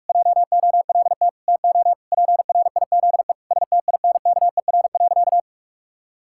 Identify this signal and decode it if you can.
Morse code, heard as a keyed tone, about 41 wpm, 700 Hz. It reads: JOLT TO PRIZE STINKER.
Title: Pangram Challenge v2 - 45wpm